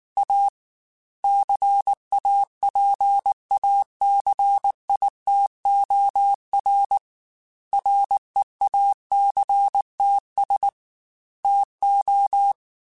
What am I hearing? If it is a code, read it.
Morse code, heard as a keyed tone, about 19 wpm, 800 Hz. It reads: A CAPACITOR REACTS TO
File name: Technical Subjects / Technical Subjects-mp3s / XC @18